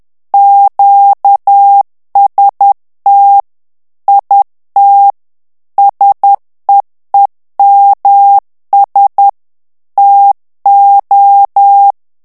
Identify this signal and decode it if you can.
Morse code, heard as a keyed tone, about 10 wpm, 800 Hz. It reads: QST IT SEEMS TO
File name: ANGLAIS texte clair 3 vitesse 10 mots minute